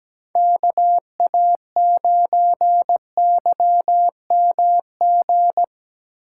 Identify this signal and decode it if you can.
Morse code, heard as a keyed tone, about 17 wpm, 700 Hz. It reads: KA9YMG
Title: K1USN SST QSO Practice - 17wpm